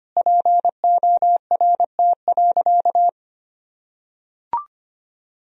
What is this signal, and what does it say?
Morse code, heard as a keyed tone, about 25 wpm, 700 Hz. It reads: PORT.
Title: Sentences from Top 1000 Words - Review Part 1 - 25wpm